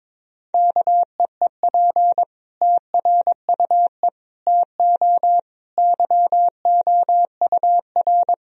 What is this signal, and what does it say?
Morse code, heard as a keyed tone, about 22 wpm, 700 Hz. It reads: KEEP TRUE TO YOUR
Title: Sentences from Top 1000 Words - Review Part 3 - 22wpm